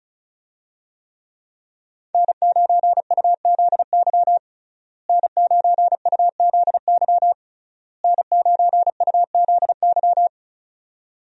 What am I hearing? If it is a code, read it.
Morse code, heard as a keyed tone, about 34 wpm, 700 Hz. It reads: N9UZY N9UZY N9UZY
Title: USA Call Signs - 5 Characters - Repeated 3x - 35wpm